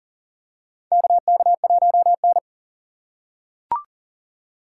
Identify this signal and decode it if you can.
Morse code, heard as a keyed tone, about 40 wpm, 700 Hz. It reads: KK1N
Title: USA Call Signs - 4 Characters - 2x Character Spacing - 40wpm